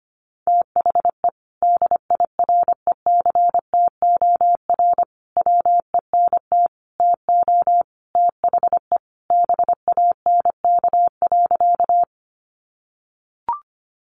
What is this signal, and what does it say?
Morse code, heard as a keyed tone, about 25 wpm, 700 Hz. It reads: THE DIRECTOR WENT TO THE BANK.
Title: Sentences from Top 1300 Words 25wpm